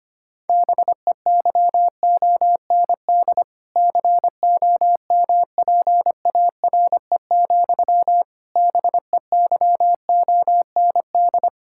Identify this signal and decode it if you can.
Morse code, heard as a keyed tone, about 25 wpm, 700 Hz. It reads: BEYOND COMPARE, BEYOND
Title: Sentences from Top 1600 Words 25wpm